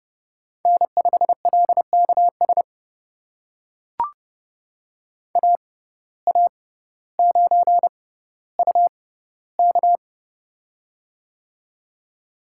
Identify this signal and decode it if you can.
Morse code, heard as a keyed tone, about 30 wpm, spaced as for 15 wpm, 700 Hz. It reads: N5LKS AA9UK
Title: USA Call Signs - 5 Characters - 6x Character Spacing - 30wpm